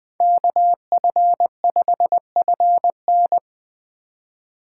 Morse code, heard as a keyed tone, about 24 wpm, 700 Hz. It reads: KF5FN
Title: Speed Racing - USA 2x2 Call Signs 20wpm